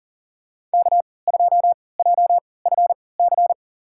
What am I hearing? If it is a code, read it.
Morse code, heard as a keyed tone, about 38 wpm, spaced as for 22 wpm, 700 Hz. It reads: K2JFC
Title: USA Call Signs - 5 Characters - 3x Character Spacing - 40wpm